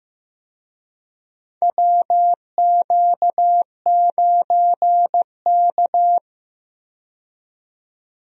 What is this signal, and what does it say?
Morse code, heard as a keyed tone, about 15 wpm, 700 Hz. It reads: WQ9K
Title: USA Call Signs - 4 Characters - 15wpm